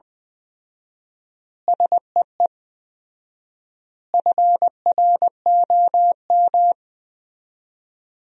Morse code, heard as a keyed tone, about 21 wpm, 700 Hz. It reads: SEE FROM
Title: 4x Word-Spacing - Sentences from Top 100 Words - 20wpm